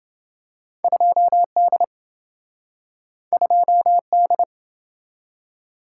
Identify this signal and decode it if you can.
Morse code, heard as a keyed tone, about 27 wpm, 700 Hz. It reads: 2D 2D